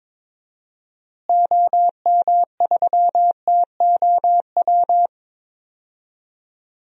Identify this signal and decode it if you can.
Morse code, heard as a keyed tone, about 22 wpm, 700 Hz. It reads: OM3TOW
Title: DXCC-100 Call Signs - 6 Characters - Repeated 2x - 22wpm